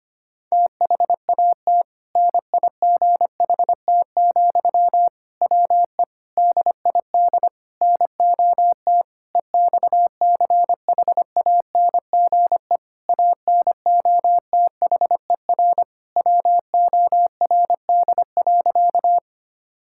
Morse code, heard as a keyed tone, about 25 wpm, 700 Hz. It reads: THAT NIGHT, WE DID NOT EXCHANGE ANOTHER WORD.
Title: Sentences from Top 1000 Words - Review Part 6 - 25wpm